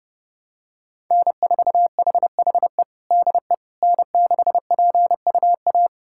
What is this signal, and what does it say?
Morse code, heard as a keyed tone, about 30 wpm, 700 Hz. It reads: N4HHE DE N6PUA
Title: USA Callsign DE Callsign - 5 Characters - 30wpm